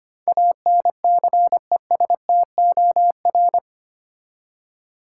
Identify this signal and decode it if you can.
Morse code, heard as a keyed tone, about 25 wpm, 700 Hz. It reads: ANCESTOR
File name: 8 Letter Words - First 3 Letters Given - 25wpm